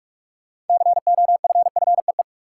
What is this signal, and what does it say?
Morse code, heard as a keyed tone, about 46 wpm, 700 Hz. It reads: COFFEE
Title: Six Letter Words - 45wpm